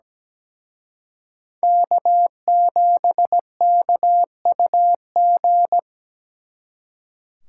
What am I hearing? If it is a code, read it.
Morse code, heard as a keyed tone, about 20 wpm, 700 Hz. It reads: K7KUG
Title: Speed Racing - USA 1x3 Call Signs 17wpm